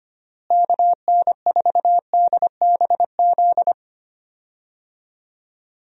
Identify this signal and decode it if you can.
Morse code, heard as a keyed tone, about 25 wpm, 700 Hz. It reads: KN4DBZ
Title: USA Call Signs - 6 Characters - Repeated 2x - 25wpm